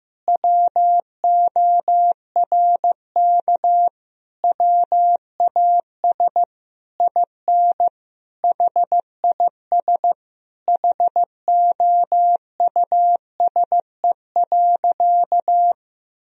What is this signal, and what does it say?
Morse code, heard as a keyed tone, about 16 wpm, 700 Hz. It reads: WORK WAS IN HIS HOUSE.
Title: Sentences from Top 1000 Words - Review Part 2 - 15wpm